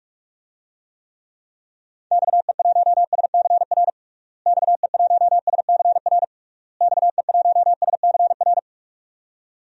Morse code, heard as a keyed tone, about 46 wpm, 700 Hz. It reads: XE1SCR XE1SCR XE1SCR
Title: DXCC-100 Call Signs - 6 Characters - Repeated 3x - 45wpm